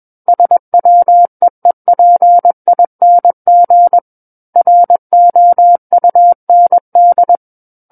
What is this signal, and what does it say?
Morse code, heard as a keyed tone, about 21 wpm, 700 Hz. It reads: SWEEPING ROUND
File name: Book One / 12-Chapter 12 - 21 WPM